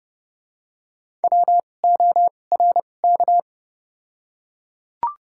Morse code, heard as a keyed tone, about 29 wpm, spaced as for 22 wpm, 700 Hz. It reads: WORK
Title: 2x Character Spacing - Top 300 Words - 30wpm